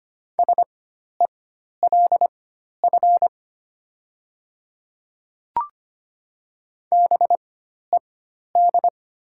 Morse code, heard as a keyed tone, about 25 wpm, spaced as for 12 wpm, 700 Hz. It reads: SELF BED
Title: 4x Character Spacing - Top 500 Words - 25wpm